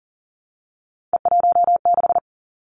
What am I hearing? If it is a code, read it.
Morse code, heard as a keyed tone, about 40 wpm, 700 Hz. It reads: E16
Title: Three Challenging Letter-Number Combinations 40wpm